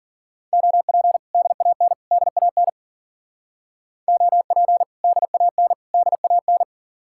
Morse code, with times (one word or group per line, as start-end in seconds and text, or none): 0.5-1.2 OP
1.3-1.9 DAN
2.1-2.7 DAN
4.1-4.8 OP
5.0-5.7 DAN
5.9-6.6 DAN